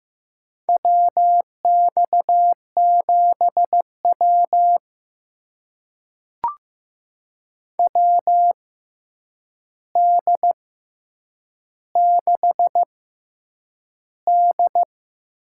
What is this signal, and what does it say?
Morse code, heard as a keyed tone, about 15 wpm, spaced as for 8 wpm, 700 Hz. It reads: WX7W WD6D
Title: USA Call Signs - 6x Character Spacing - 15wpm